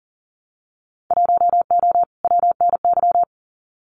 Morse code, heard as a keyed tone, about 39 wpm, 700 Hz. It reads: 1O WNY
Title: Winter Field Day Exchanges Encore 40wpm